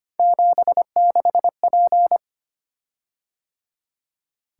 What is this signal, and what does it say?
Morse code, heard as a keyed tone, about 25 wpm, 700 Hz. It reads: 76P
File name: Three Letter-Number Combinations - 25wpm